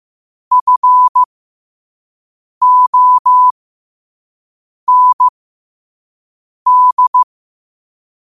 Morse code, heard as a keyed tone, about 15 wpm, spaced as for 5 wpm, 1000 Hz. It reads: FOND
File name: qotdfiles / 5wpm